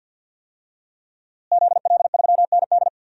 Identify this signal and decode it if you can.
Morse code, heard as a keyed tone, about 48 wpm, 700 Hz. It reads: ZL3ND
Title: DXCC-100 Call Signs - 5 Characters - Repeated 3x - 50wpm